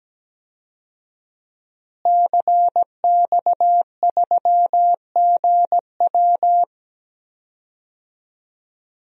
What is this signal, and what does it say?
Morse code, heard as a keyed tone, about 17 wpm, 700 Hz. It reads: CX3GW
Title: DXCC-100 Call Signs - 5 Characters - Repeated 2x - 17wpm